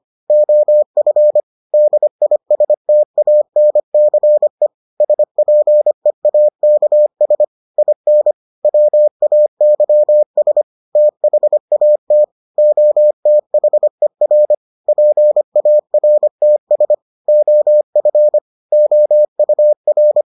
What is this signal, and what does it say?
Morse code, heard as a keyed tone, about 25 wpm, 600 Hz. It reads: OF DISTANCE SPEAKS IN WAYS THAT OTHER PARTS OF OUR